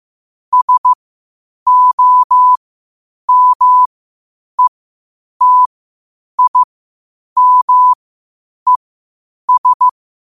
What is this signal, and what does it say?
Morse code, heard as a keyed tone, about 15 wpm, spaced as for 8 wpm, 1000 Hz. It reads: SOMETIMES
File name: qotdfiles / 8wpm